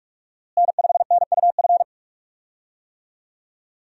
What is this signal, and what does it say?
Morse code, heard as a keyed tone, about 46 wpm, 700 Hz. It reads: N5NUF
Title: USA Call Signs - 5 Characters - Repeated 3x - 45wpm